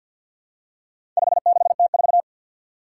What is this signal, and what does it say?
Morse code, heard as a keyed tone, about 47 wpm, 700 Hz. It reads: 56T4
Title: Four Challenging Letter-Number Combinations 50wpm